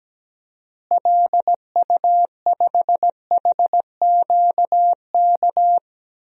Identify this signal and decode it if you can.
Morse code, heard as a keyed tone, about 17 wpm, 700 Hz. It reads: LU5HQK
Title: DXCC-100 Call Signs - 6 Characters - Repeated 2x - 17wpm